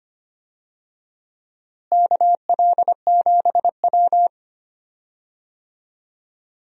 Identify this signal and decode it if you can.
Morse code, heard as a keyed tone, about 25 wpm, 700 Hz. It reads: KL7W
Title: DXCC-100 Call Signs - 4 Characters - Repeated 3x - 25wpm